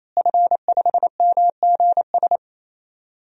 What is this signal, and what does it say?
Morse code, heard as a keyed tone, about 28 wpm, 700 Hz. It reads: F5MGS
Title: DXCC-100 Call Signs - 5 Characters - Repeated 3x - 28wpm